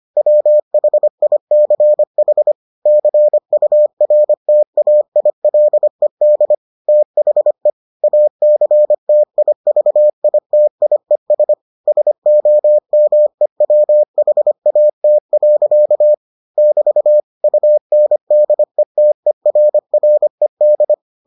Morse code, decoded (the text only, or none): WHICH CURTAILED THE ACTIVITIES SOMEWHAT. <BT> UNDETERRED